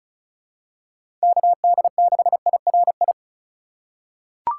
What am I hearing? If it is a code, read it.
Morse code, heard as a keyed tone, about 34 wpm, 700 Hz. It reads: KD6IRI
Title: K1USN SST QSO Practice - 35wpm